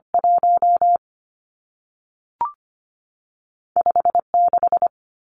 Morse code, heard as a keyed tone, about 25 wpm, 700 Hz. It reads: 1 56
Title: Numbers - 2-Digit 25wpm